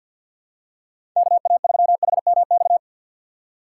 Morse code, heard as a keyed tone, about 45 wpm, 700 Hz. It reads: KA3HMX
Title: USA Call Signs - 4x Character Spacing - 50wpm